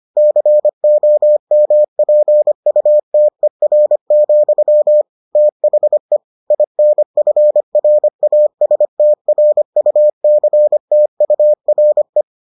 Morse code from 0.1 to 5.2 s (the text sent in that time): COMPUTER,